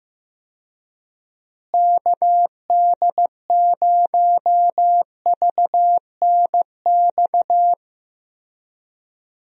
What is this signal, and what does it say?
Morse code, heard as a keyed tone, about 15 wpm, 700 Hz. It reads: KD0VNX
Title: USA Call Signs - 6 Characters - Repeated 2x - 15wpm